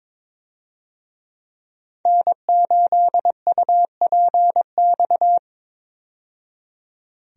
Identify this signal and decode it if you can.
Morse code, heard as a keyed tone, about 22 wpm, 700 Hz. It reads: N8UPX
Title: USA Call Signs - 5 Characters - 22wpm